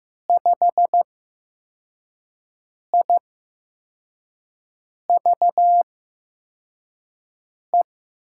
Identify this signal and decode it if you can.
Morse code, heard as a keyed tone, about 15 wpm, spaced as for 4 wpm, 700 Hz. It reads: 5IVE
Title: USA Call Signs - 6 Characters - 8x Character Spacing - 15wpm